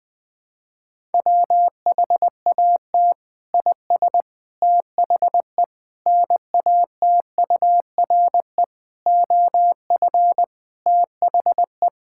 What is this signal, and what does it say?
Morse code, heard as a keyed tone, about 20 wpm, 700 Hz. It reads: WHAT IS THE NATURE OF THE
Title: Sentences from Top 1000 Words - Review Part 5 - 20wpm